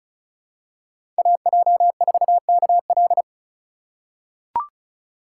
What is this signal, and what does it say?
Morse code, heard as a keyed tone, about 36 wpm, 700 Hz. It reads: AJ4KL
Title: USA Call Signs - 5 Characters - 4x Character Spacing - 35wpm